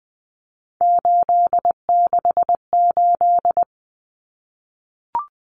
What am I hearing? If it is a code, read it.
Morse code, heard as a keyed tone, about 20 wpm, 700 Hz. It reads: 868
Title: Numbers - 3-Digit 20wpm